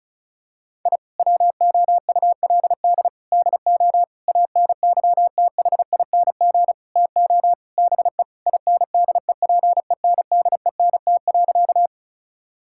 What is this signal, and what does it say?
Morse code, heard as a keyed tone, about 34 wpm, 700 Hz. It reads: I WOULD DO ANYTHING TO BE INDEPENDENT.
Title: Sentences from Top 1000 Words - Review Part 6 - 35wpm